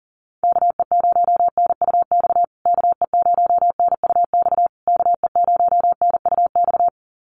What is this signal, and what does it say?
Morse code, heard as a keyed tone, about 40 wpm, 700 Hz. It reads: KE0NUX KE0NUX KE0NUX
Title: USA Call Signs Repeated 3x 40wpm